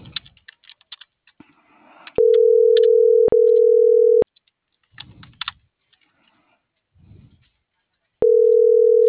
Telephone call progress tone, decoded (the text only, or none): ringback tone